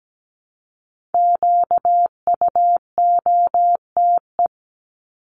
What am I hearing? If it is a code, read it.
Morse code, heard as a keyed tone, about 17 wpm, 700 Hz. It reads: QUOTE
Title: Top 1751-2000 Words 17wpm